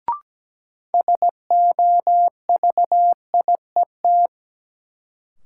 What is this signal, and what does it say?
Morse code, heard as a keyed tone, about 17 wpm, 700 Hz. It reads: SOVIET
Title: Sets of 5 Words Encore - New Word Drill - 17wpm